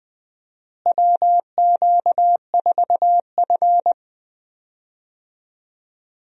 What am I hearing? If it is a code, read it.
Morse code, heard as a keyed tone, about 20 wpm, 700 Hz. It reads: WQ4F
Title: USA Call Signs - 4 Characters - Repeated 2x - 20wpm